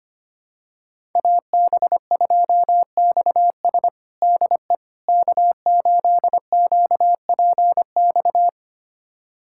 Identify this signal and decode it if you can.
Morse code, heard as a keyed tone, about 25 wpm, 700 Hz. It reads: AB2XS DE K8QPX
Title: USA Callsign DE Callsign - 5 Characters - 25wpm